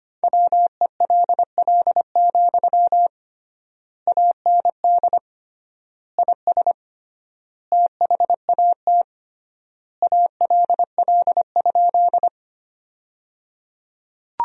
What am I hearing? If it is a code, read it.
Morse code, heard as a keyed tone, about 25 wpm, 700 Hz. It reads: WELL, AND IS THAT ALL?
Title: 3x Word-Spacing - Sentences from Top 200 Words - 25wpm